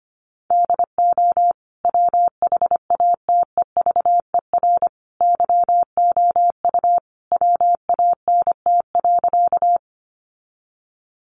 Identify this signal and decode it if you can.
Morse code, heard as a keyed tone, about 25 wpm, 700 Hz. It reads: DO WHATEVER YOU WANT.
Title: Sentences from Top 1100 Words 25wpm